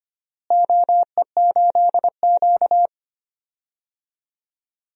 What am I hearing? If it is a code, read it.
Morse code, heard as a keyed tone, about 25 wpm, 700 Hz. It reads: OE8Q
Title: DXCC-100 Call Signs - 4 Characters - Repeated 3x - 25wpm